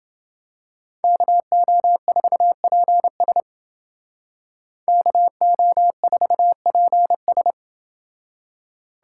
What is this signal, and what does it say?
Morse code, heard as a keyed tone, about 29 wpm, 700 Hz. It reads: KO4PS KO4PS